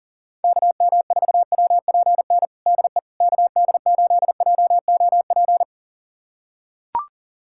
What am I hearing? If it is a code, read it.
Morse code, heard as a keyed tone, about 37 wpm, 700 Hz. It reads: KM4WPN DE KD8JOP
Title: USA Callsign DE Callsign - 6 Characters - 40wpm